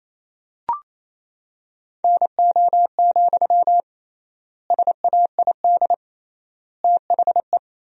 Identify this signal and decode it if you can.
Morse code, heard as a keyed tone, about 27 wpm, 700 Hz. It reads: NO, SAID THE
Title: 3x Word-Spacing - Sentences from Top 300 Words - 28wpm